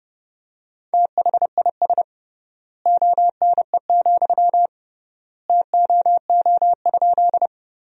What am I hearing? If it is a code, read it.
Morse code, heard as a keyed tone, about 30 wpm, 700 Hz. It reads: THIS ONE, TOO?
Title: 3x Word-Spacing - Sentences from Top 200 Words - 30wpm